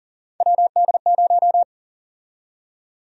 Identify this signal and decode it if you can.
Morse code, heard as a keyed tone, about 40 wpm, 700 Hz. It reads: WD0
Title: USA Call Sign Prefixes - Repeated 2x - 40wpm